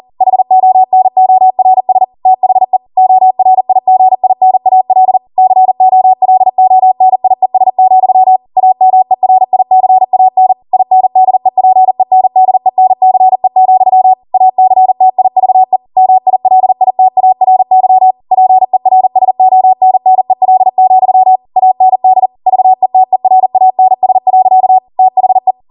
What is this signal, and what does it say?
Morse code, heard as a keyed tone, about 40 wpm, 750 Hz. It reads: HONORS THE ORIGINAL COLONIES, AMERICAN INDEPENDENCE, ACTIVE MILITARY PERSONNEL, AND VETERANS. THE